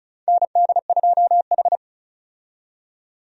Morse code, heard as a keyed tone, about 36 wpm, 700 Hz. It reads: ND2H
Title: USA Call Signs - 4 Characters - 35wpm